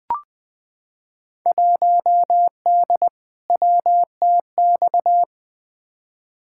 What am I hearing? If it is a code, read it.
Morse code, heard as a keyed tone, about 20 wpm, 700 Hz. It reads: 1D WTX
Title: Field Day Exchanges - 20wpm